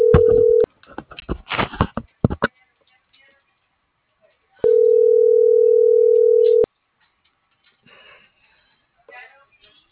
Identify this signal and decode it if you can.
Ringback tone (North America), 440 + 480 Hz.